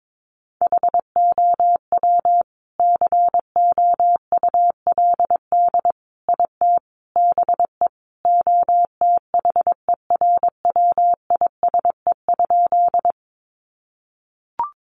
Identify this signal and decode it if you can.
Morse code, heard as a keyed tone, about 22 wpm, 700 Hz. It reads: HOW COULD IT BE OTHERWISE?
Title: Sentences from Top 1200 Words 22wpm